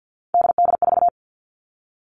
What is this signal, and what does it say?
Morse code, heard as a keyed tone, about 50 wpm, 700 Hz. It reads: DD4
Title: Three Challenging Letter-Number Combinations 50wpm